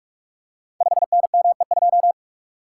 Morse code, heard as a keyed tone, about 44 wpm, 700 Hz. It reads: 5NME2
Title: Code Groups - 5 Letter:Numbers - 45wpm